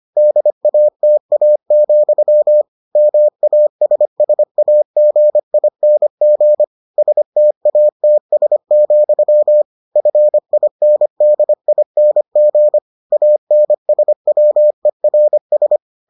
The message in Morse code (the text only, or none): DATA, MASSAGING STATS, FINDING ANSWERS